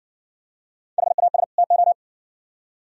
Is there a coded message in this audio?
Yes: Morse code from 1.0 to 1.9 s, heard as a keyed tone, about 56 wpm, 700 Hz. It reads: HAS TO